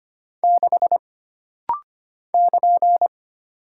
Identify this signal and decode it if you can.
Morse code, heard as a keyed tone, about 25 wpm, 700 Hz. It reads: 6 <KN>
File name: Single Character (Letter:Number:Punctuation:Prosign) - Rapid-Fire - 25wpm